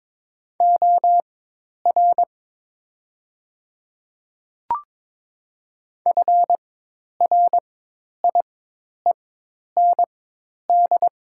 Morse code, heard as a keyed tone, about 22 wpm, spaced as for 10 wpm, 700 Hz. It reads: OR FRIEND
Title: 4x Character Spacing - Top 300 Words - 22wpm